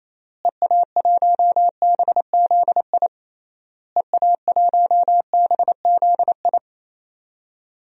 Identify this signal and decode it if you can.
Morse code, heard as a keyed tone, about 27 wpm, 700 Hz. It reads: EA1BZI EA1BZI
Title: DXCC-100 Call Signs - 6 Characters - Repeated 3x - 28wpm